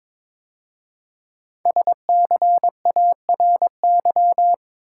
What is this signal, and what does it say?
Morse code, heard as a keyed tone, about 22 wpm, 700 Hz. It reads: SCARY
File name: 5 Letter Words - First 3 Letters Given - 22wpm